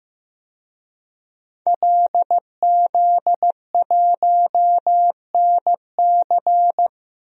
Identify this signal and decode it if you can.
Morse code, heard as a keyed tone, about 15 wpm, 700 Hz. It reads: LZ1NC
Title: DXCC-100 Call Signs - 5 Characters - Repeated 2x - 15wpm